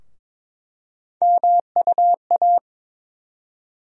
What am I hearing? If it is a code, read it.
Morse code, heard as a keyed tone, about 22 wpm, 700 Hz. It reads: MUA
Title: USA Call Sign Postfixes - 3 Letters 22wpm